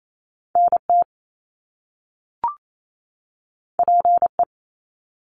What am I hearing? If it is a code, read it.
Morse code, heard as a keyed tone, about 28 wpm, 700 Hz. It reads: NT PE
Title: Canadian Provinces and Territories 28wpm